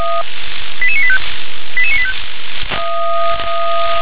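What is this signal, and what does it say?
Five-tone selective call (ZVEI), heard as quick runs of five tones, tones of about 70 ms. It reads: 88985 88985
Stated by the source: ZVEI five-tone selective call received on an SDR.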